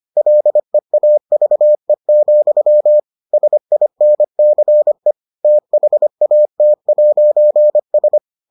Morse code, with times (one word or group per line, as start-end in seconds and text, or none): 0.2-3.0 LEAVE,
3.3-5.1 SINCE
5.4-8.2 THAT'S